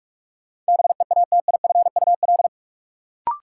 Morse code, heard as a keyed tone, about 41 wpm, 700 Hz. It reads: BEATIFUL
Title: Sentences from Top 1300 Words - New Word Drill - 45wpm